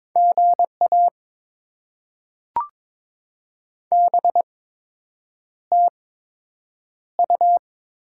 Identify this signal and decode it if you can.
Morse code, heard as a keyed tone, about 22 wpm, spaced as for 8 wpm, 700 Hz. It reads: GA BTU
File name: Top 20 QSO Elements - 8x Character Spacing - 22wpm